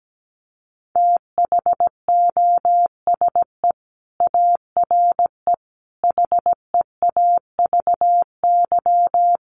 Morse code, heard as a keyed tone, about 17 wpm, 700 Hz. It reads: THOSE ARE HEAVY
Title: Sentences from Top 1100 Words 17wpm